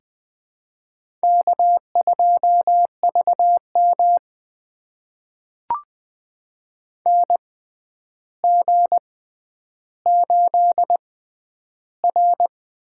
Morse code, heard as a keyed tone, about 20 wpm, spaced as for 10 wpm, 700 Hz. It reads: K2VM NG8R
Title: USA Call Signs - 4 Characters - 6x Character Spacing - 20wpm